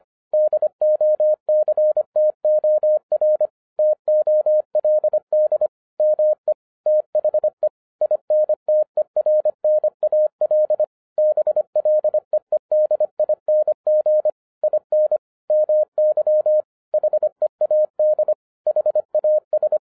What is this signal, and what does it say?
Morse code, heard as a keyed tone, about 25 wpm, 610 Hz. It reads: DOCTOR TOLD ME THE INTERNAL BLEEDING IN MY HEAD HAS